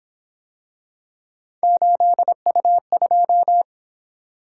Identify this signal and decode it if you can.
Morse code, heard as a keyed tone, about 27 wpm, 700 Hz. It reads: 8U2
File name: Speed-Racing - Three Letter-Number Combinations - 22wpm